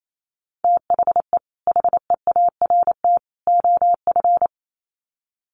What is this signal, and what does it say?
Morse code, heard as a keyed tone, about 28 wpm, 700 Hz. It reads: THE HEART OF
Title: Sets of 3 Words 28wpm